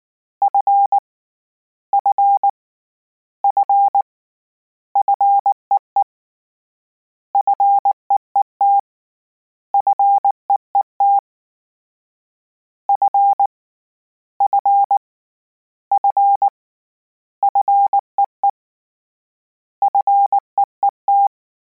Morse code, heard as a keyed tone, about 19 wpm, 800 Hz. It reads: F F F FEE FEET FEET F F F FEE FEET